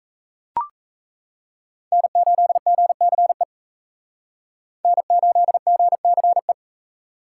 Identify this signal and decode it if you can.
Morse code, heard as a keyed tone, about 37 wpm, 700 Hz. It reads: N8GCE N8GCE